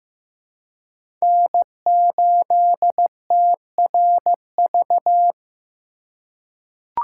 Morse code, heard as a keyed tone, about 15 wpm, 700 Hz. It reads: N8TRV
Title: USA Call Signs - 5 Characters - 4x Character Spacing - 15wpm